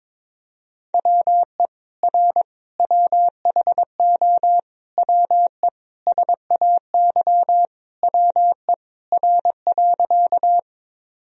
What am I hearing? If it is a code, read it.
Morse code, heard as a keyed tone, about 22 wpm, 700 Hz. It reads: WE R WHO WE SAY WE R.
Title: Sentences from Top 200 Words and Abbreviations 22wpm